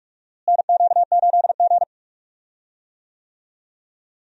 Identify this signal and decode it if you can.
Morse code, heard as a keyed tone, about 46 wpm, 700 Hz. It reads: NQ8G
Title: USA Call Signs - 4 Characters - Repeated 2x - 45wpm